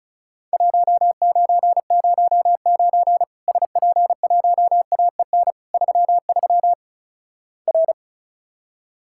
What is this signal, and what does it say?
Morse code, heard as a keyed tone, about 36 wpm, 700 Hz. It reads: 1909 SP1AEN 33 R
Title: Worked All Europe DX Contest - QTC Exchange - 35wpm